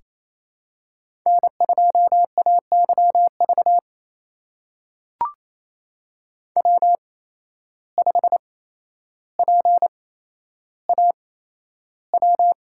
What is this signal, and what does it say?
Morse code, heard as a keyed tone, about 28 wpm, spaced as for 12 wpm, 700 Hz. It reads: N2AYV W5PAW
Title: USA Call Signs - 5 Characters - 8x Character Spacing - 28wpm